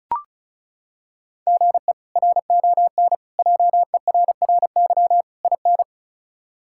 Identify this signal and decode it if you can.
Morse code, heard as a keyed tone, about 36 wpm, 700 Hz. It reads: GE RON JERRY IN
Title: K1USN SST QSO Practice - 35wpm